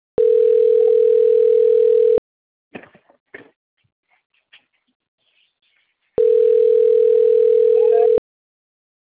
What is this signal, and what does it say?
Ringback tone (North America), 440 + 480 Hz.